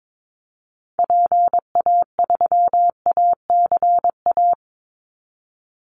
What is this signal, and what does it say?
Morse code, heard as a keyed tone, about 22 wpm, 700 Hz. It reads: PA3ACA
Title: Call Signs Repeated 3x - 22wpm